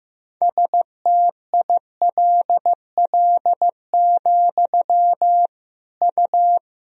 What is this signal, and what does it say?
Morse code, heard as a keyed tone, about 16 wpm, 700 Hz. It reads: STILL, U
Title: Sentences from Top 300 Words and Abbreviations 15wpm